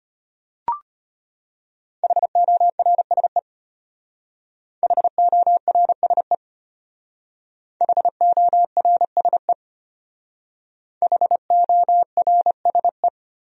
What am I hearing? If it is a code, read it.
Morse code, heard as a keyed tone, about 32 wpm, 700 Hz. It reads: HORSE HORSE HORSE HORSE